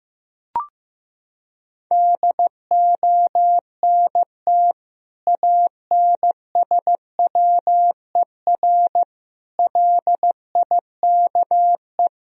Morse code, heard as a keyed tone, about 16 wpm, 700 Hz. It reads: DONT ANSWER LIKE
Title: Sentences from Top 400 Words and Abbreviations 15wpm